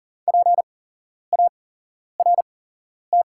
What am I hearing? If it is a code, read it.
Morse code, heard as a keyed tone, about 37 wpm, spaced as for 11 wpm, 700 Hz. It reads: PART
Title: 8x Character Spacing - Top 300 Words - 40wpm